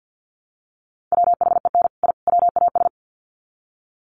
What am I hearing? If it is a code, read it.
Morse code, heard as a keyed tone, about 50 wpm, 700 Hz. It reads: WHEN I WAS
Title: Sets of 3 Words 50wpm